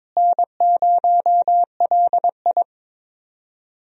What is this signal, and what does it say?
Morse code, heard as a keyed tone, about 22 wpm, 700 Hz. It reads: N0LI
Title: USA Call Signs - 4 Characters - Repeated 3x - 22wpm